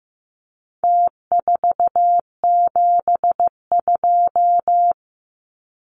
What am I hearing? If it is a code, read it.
Morse code, heard as a keyed tone, about 15 wpm, 700 Hz. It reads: T472
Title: Numbers - Contest Style - 4-Digit 15wpm